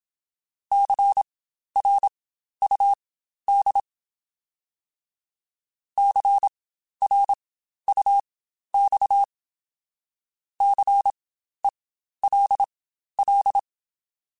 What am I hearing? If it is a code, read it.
Morse code, heard as a keyed tone, about 27 wpm, spaced as for 12 wpm, 800 Hz. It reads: CRUD CRUX CELL